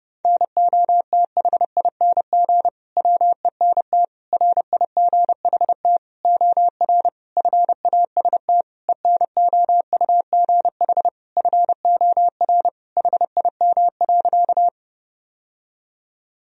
Morse code, heard as a keyed tone, about 32 wpm, 700 Hz. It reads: NOTHING WENT RIGHT OR FAST ENOUGH FOR HIM.
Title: Sentences from Top 1000 Words - Review Part 2 - 30wpm